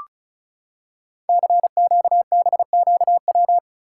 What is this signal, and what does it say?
Morse code, heard as a keyed tone, about 36 wpm, 700 Hz. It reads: CQBQW
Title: Code Groups - 5 Letter:Numbers - 35wpm